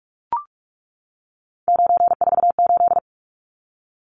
Morse code, heard as a keyed tone, about 44 wpm, 700 Hz. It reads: 948
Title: Numbers - 3-Digit 45wpm